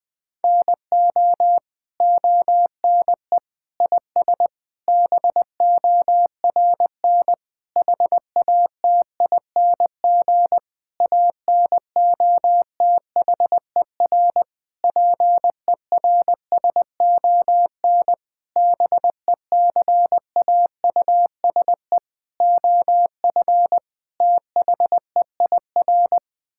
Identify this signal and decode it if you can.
Morse code, heard as a keyed tone, about 20 wpm, 700 Hz. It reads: NO ONE IS BORN HATING ANOTHER PERSON BECAUSE OF THEIR
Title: Sentences with 9-Word Phrases - 20wpm